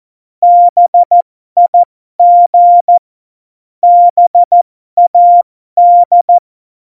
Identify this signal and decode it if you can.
Morse code, heard as a keyed tone, about 14 wpm, spaced as for 12 wpm, 700 Hz. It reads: BIG BAD
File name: Morsekurs / Witze